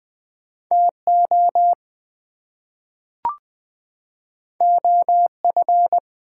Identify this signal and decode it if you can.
Morse code, heard as a keyed tone, about 20 wpm, 700 Hz. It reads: TO OF
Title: Two Letter Words - 20wpm